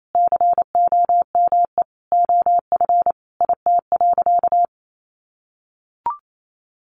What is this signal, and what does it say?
Morse code, heard as a keyed tone, about 28 wpm, 700 Hz. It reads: COME OF IT.
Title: Sentences from Top 1100 Words 28wpm